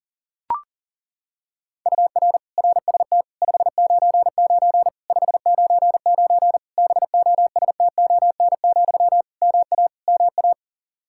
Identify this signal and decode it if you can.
Morse code, heard as a keyed tone, about 40 wpm, 700 Hz. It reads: UR RST 599 599 BOSTON, MA MA
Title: Practice QSO - Exchange 1 - Beginning - 40wpm